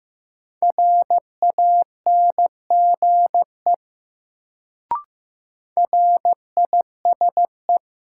Morse code, heard as a keyed tone, about 15 wpm, 700 Hz. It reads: RANGE RISE
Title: Sentences from Top 900 Words - New Word Drill - 15wpm